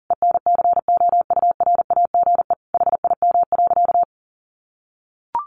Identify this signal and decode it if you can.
Morse code, heard as a keyed tone, about 40 wpm, 700 Hz. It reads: ENCOURAGE HIM.
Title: Sentences from Top 1400 Words 40wpm